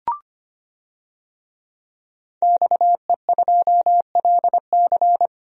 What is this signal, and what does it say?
Morse code, heard as a keyed tone, about 25 wpm, 700 Hz. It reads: XE2LC
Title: DXCC-100 Call Signs - 5 Characters - 25wpm